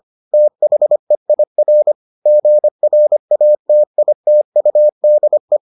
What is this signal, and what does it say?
Morse code, heard as a keyed tone, about 25 wpm, 600 Hz. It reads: THEIR GRATITUDE